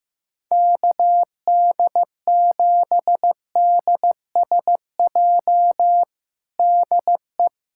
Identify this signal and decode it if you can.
Morse code, heard as a keyed tone, about 15 wpm, 700 Hz. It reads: KD7DSJ DE
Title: USA Callsign DE Callsign 15wpm